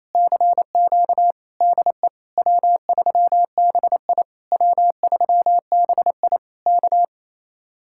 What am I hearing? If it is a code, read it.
Morse code, heard as a keyed tone, about 27 wpm, 700 Hz. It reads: CQ DE W3BI W3BI K
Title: Speed-Racing Common QSO Phrases - 28wpm